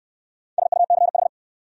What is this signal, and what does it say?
Morse code, heard as a keyed tone, about 62 wpm, 700 Hz. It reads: SUCH